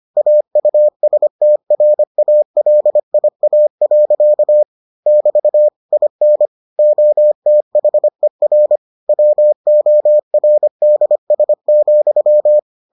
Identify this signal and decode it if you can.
Morse code, heard as a keyed tone, about 25 wpm, 600 Hz. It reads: AUSTRALIA. <BT> IN OTHER WORDS,